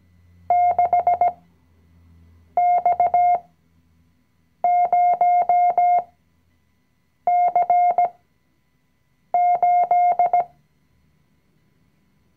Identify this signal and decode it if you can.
Morse code, heard as a keyed tone, about 17 wpm, spaced as for 6 wpm, 710 Hz. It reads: 6X0C8
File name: CW-Online-Kurs / Anfänger / Lektion 020